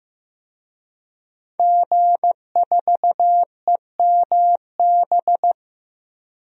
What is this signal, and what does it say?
Morse code, heard as a keyed tone, about 16 wpm, 700 Hz. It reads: G4EMB
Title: DXCC-100 Call Signs - 5 Characters - 15wpm